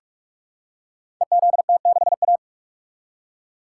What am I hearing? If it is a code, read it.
Morse code, heard as a keyed tone, about 47 wpm, 700 Hz. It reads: EZT6A
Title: Code Groups - 5 Letter:Numbers - 45wpm